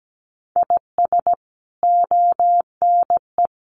Morse code, heard as a keyed tone, about 17 wpm, 700 Hz. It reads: IS ONE
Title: Sets of 5 Words 17wpm